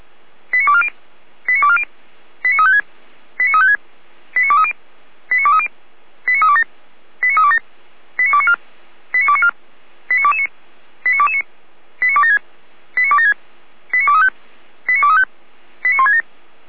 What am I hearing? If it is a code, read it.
Five-tone selective call (ZVEI), heard as quick runs of five tones, tones of about 70 ms. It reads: 89240 89240 89356 89356 89230 89230 89327 89327 89294 89294 89229 89229 89276 89276 89235 89235 89167
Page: /455/SDR/ZVEI/